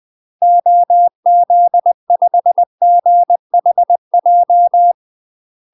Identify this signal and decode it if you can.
Morse code, heard as a keyed tone, about 20 wpm, 700 Hz. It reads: OZ5GHJ
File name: UK MASTER QSOs TRAINING / 20WPM / UK MASTER QSOs 0045